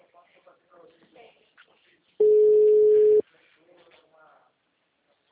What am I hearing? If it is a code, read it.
Ringback tone (Europe, 425 Hz), with speech.